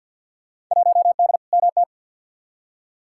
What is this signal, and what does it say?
Morse code, heard as a keyed tone, about 51 wpm, 700 Hz. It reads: 1D MT
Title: Field Day Exchanges - 50wpm